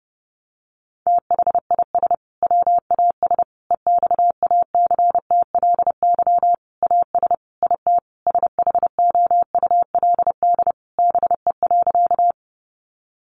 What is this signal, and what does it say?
Morse code, heard as a keyed tone, about 30 wpm, 700 Hz. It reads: THIS WAS EXACTLY AS IT SHOULD BE.
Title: Sentences from Top 900 Words 30wpm